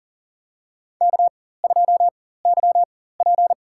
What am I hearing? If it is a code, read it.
Morse code, heard as a keyed tone, about 40 wpm, spaced as for 19 wpm, 700 Hz. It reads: K2YP
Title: USA Call Signs - 4 Characters - 4x Character Spacing - 40wpm